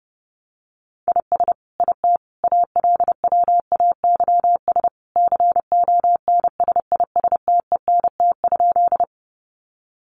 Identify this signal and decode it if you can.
Morse code, heard as a keyed tone, about 30 wpm, 700 Hz. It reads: IS IT ALWAYS CONSISTENT?
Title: Sentences from Top 1500 Words 30wpm